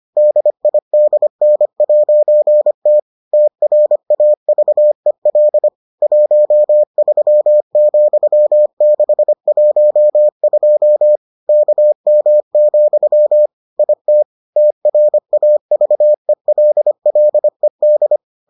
Morse code, heard as a keyed tone, about 25 wpm, 600 Hz. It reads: DIDN'T TRAVEL 13,612 KM, IT TRAVELLED